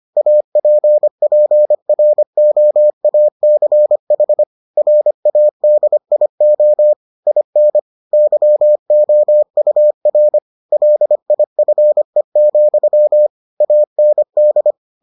Morse code, heard as a keyed tone, about 25 wpm, 600 Hz. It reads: APPROACH RADIO IN YOUR LIFE, AND